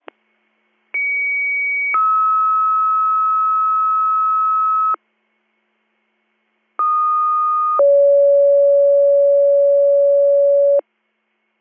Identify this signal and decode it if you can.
Two-tone page, heard as a one-second tone, then a three-second tone, about 2250 then 1290 Hz, about 1230 then 560 Hz. It is heard from 0.9 to 10.8 s.